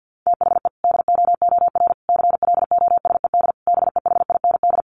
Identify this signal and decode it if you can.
Morse code, heard as a keyed tone, about 50 wpm, 700 Hz. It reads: THE DOOR CLOSED BEHIND
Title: Sentences from Top 1400 Words 50wpm